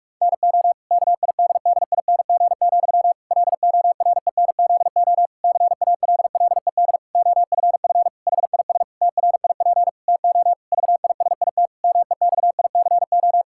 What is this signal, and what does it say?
Morse code, heard as a keyed tone, about 47 wpm, 700 Hz. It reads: NO KIDDING, LORENZO CALLED OFF HIS TRIP TO VISIT MEXICO
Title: Pangram Challenge v2 - 45wpm